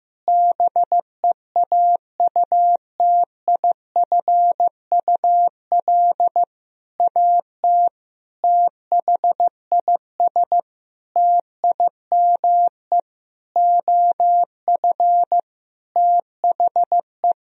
Morse code, heard as a keyed tone, about 15 wpm, 700 Hz. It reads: BEAUTIFUL AT THIS TIME OF THE
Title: Sentences from Top 1000 Words - Review Part 1 - 15wpm